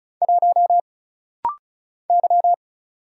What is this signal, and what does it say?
Morse code, heard as a keyed tone, about 36 wpm, 700 Hz. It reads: 1 Y
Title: Single Letter-Number - Rapid-Fire 35wpm